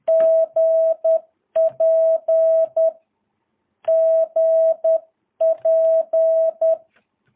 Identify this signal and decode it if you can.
Morse code, heard as a keyed tone, about 10 wpm, 650 Hz. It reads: GP GP
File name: jh8cbh / 2022cw